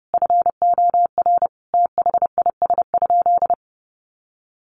Morse code, heard as a keyed tone, about 30 wpm, 700 Hz. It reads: FOR THIS?
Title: Sentences from Top 1300 Words 30wpm